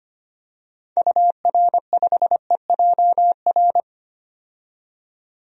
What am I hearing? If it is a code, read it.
Morse code, heard as a keyed tone, about 25 wpm, 700 Hz. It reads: UR5EJR
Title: DXCC-100 Call Signs - 6 Characters - 25wpm